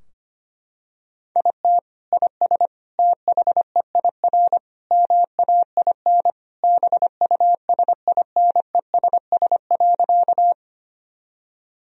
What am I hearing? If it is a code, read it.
Morse code, heard as a keyed tone, about 25 wpm, 700 Hz. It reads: IT IS THEIR MAIN BUSINESS.
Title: Sentences from Top 1000 Words - Review Part 2 - 25wpm